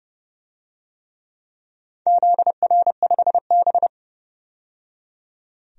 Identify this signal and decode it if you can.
Morse code, heard as a keyed tone, about 31 wpm, 700 Hz. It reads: ZR5B
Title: DXCC-100 Call Signs - 4 Characters - 30wpm